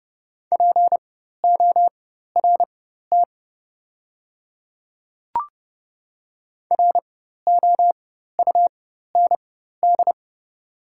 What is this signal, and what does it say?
Morse code, heard as a keyed tone, about 30 wpm, spaced as for 14 wpm, 700 Hz. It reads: PORT ROUND
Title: 4x Character Spacing - Top 200 Words - 30wpm